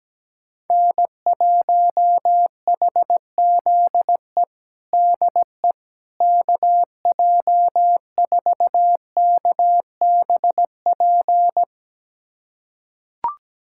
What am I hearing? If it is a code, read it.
Morse code, heard as a keyed tone, about 17 wpm, 700 Hz. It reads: N1HZE DE KJ4KBP
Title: USA Callsign DE Callsign 17wpm